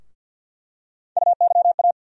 Morse code, heard as a keyed tone, about 48 wpm, 700 Hz. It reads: UYA
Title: USA Call Sign Postfixes - 3 Letters 50wpm